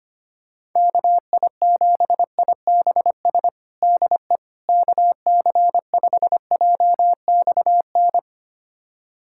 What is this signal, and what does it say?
Morse code, heard as a keyed tone, about 25 wpm, 700 Hz. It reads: KI7IBS DE KC5JXN
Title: USA Callsign DE Callsign 25wpm